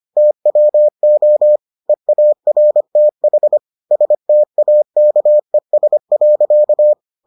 Morse code, heard as a keyed tone, about 25 wpm, 600 Hz. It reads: TWO EARTH STAKES.